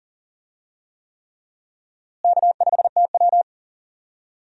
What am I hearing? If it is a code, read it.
Morse code, heard as a keyed tone, about 40 wpm, 700 Hz. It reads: K5TW
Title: USA Call Signs - 4 Characters - Repeated 3x - 40wpm